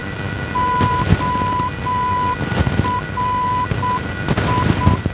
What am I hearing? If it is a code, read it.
Morse code, heard as a keyed tone, about 7 wpm, 1000 Hz, with noise. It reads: ORI
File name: ORI NDB 1988